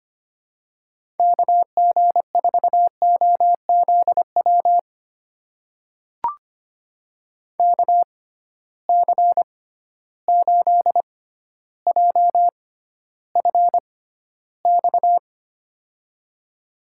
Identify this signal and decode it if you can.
Morse code, heard as a keyed tone, about 25 wpm, spaced as for 13 wpm, 700 Hz. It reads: KG4OZW KC8JFX
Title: USA Call Signs - 6 Characters - 6x Character Spacing - 25wpm